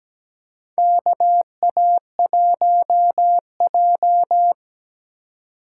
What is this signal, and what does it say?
Morse code, heard as a keyed tone, about 17 wpm, 700 Hz. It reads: KA1J
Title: USA Call Signs - 3x Character Spacing - 17wpm